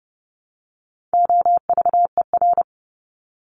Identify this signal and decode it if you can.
Morse code, heard as a keyed tone, about 30 wpm, 700 Hz. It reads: OVER
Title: Top 200 Words 30wpm